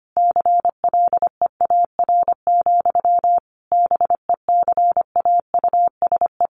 Morse code, heard as a keyed tone, about 25 wpm, 700 Hz. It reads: CLEAR, BECAUSE
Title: Sentences from Top 1000 Words 25wpm